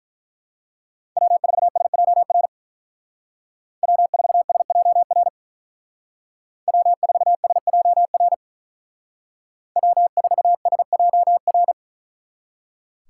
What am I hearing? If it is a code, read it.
Morse code, heard as a keyed tone, about 43 wpm, 700 Hz. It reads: W4SJR W4SJR W4SJR W4SJR
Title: Speed Racing - USA Call Signs 35wpm